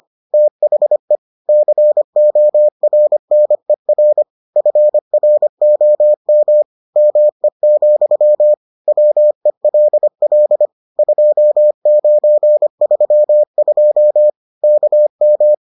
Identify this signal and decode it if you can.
Morse code, heard as a keyed tone, about 25 wpm, 600 Hz. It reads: THE CORNER FROM ME, WELL 2932 KM